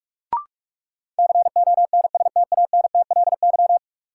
Morse code, heard as a keyed tone, about 44 wpm, 700 Hz. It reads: CONSTANTLY
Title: Sentences from Top 1600 Words - New Word Drill Alt - 45wpm